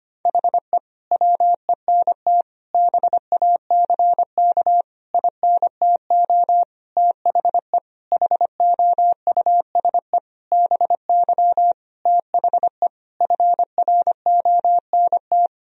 Morse code, heard as a keyed tone, about 25 wpm, 700 Hz. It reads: HE WENT BACK INTO THE HOUSE BY THE FRONT
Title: Sentences from Top 1000 Words - Review Part 2 - 25wpm